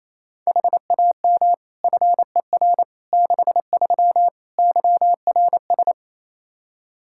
Morse code, heard as a keyed tone, about 28 wpm, 700 Hz. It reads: HAM FER 63 YRS
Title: Practice QSO - Exchange 3 - More Info - 28wpm